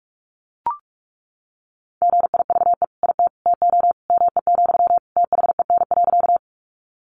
Morse code, heard as a keyed tone, about 44 wpm, 700 Hz. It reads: GIVE IT TO ME, THEN.
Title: Sentences from Top 200 Words 45wpm